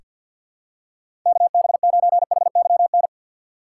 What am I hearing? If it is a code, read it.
Morse code, heard as a keyed tone, about 47 wpm, 700 Hz. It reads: KB9HYN
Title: USA Call Signs - 6 Characters - 3x Character Spacing - 50wpm